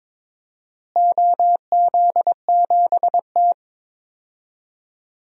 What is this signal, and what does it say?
Morse code, heard as a keyed tone, about 21 wpm, 700 Hz. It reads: OZ7T
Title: DXCC-100 Call Signs - 4 Characters - Repeated 2x - 22wpm